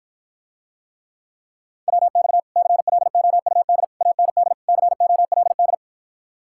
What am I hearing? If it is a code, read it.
Morse code, heard as a keyed tone, about 54 wpm, 700 Hz. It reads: WX CLOUD AND COLD